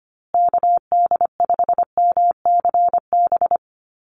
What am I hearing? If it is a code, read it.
Morse code, heard as a keyed tone, about 25 wpm, 700 Hz. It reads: KD5MCB
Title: USA Call Signs Repeated 3x 25wpm